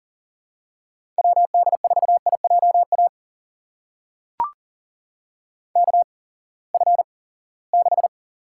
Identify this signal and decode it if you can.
Morse code, heard as a keyed tone, about 40 wpm, spaced as for 23 wpm, 700 Hz. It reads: WD4IJA KF6
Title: USA Call Signs - 6 Characters - 8x Character Spacing - 40wpm